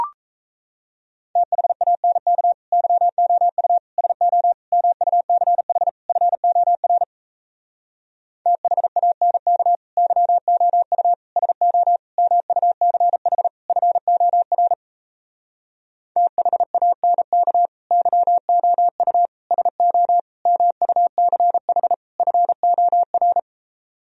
Morse code, heard as a keyed tone, about 37 wpm, 700 Hz. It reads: THANK YOU SO MUCH FOR THANK YOU SO MUCH FOR THANK YOU SO MUCH FOR